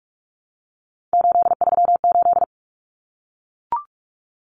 Morse code, heard as a keyed tone, about 44 wpm, 700 Hz. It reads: 838
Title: Numbers - 3-Digit 45wpm